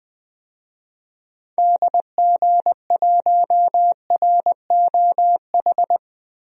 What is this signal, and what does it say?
Morse code, heard as a keyed tone, about 20 wpm, 700 Hz. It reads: DG1ROH
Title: DXCC-100 Call Signs - 6 Characters - Repeated 2x - 20wpm